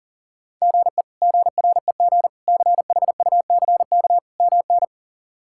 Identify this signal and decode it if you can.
Morse code, heard as a keyed tone, about 37 wpm, 700 Hz. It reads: GE GREG CHUCK MN
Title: K1USN SST QSO Practice - 40wpm